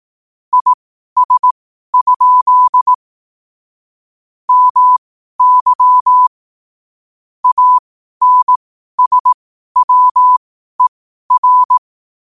Morse code, heard as a keyed tone, about 18 wpm, spaced as for 12 wpm, 1000 Hz. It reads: IS? MY ANSWER